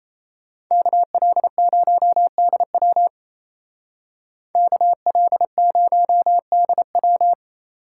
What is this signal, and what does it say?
Morse code, heard as a keyed tone, about 34 wpm, 700 Hz. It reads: KL0DW KL0DW
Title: Speed Racing - USA Call Signs 28wpm